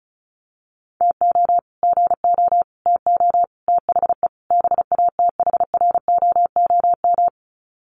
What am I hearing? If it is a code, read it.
Morse code, heard as a keyed tone, about 34 wpm, 700 Hz. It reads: TO GO TO THE BATHROOM
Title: Sets of 5 Words Encore 35wpm